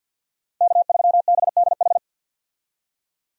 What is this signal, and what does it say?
Morse code, heard as a keyed tone, about 51 wpm, 700 Hz. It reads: K3BDH
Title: USA Call Signs - 5 Characters - 50wpm